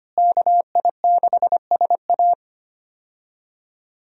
Morse code, heard as a keyed tone, about 25 wpm, 690 Hz. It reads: KI6SA
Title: USA Call Signs - 5 Characters - Repeated 2x - 25wpm